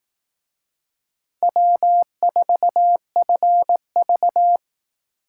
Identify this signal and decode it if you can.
Morse code, heard as a keyed tone, about 18 wpm, 700 Hz. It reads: W4FV
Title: Speed Racing - USA 1x2 Call Signs 15wpm